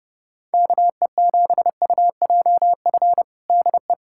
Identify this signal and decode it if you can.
Morse code, heard as a keyed tone, about 29 wpm, 700 Hz. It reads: KE7UJF DE
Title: USA Callsign DE Callsign - 6 Characters - 30wpm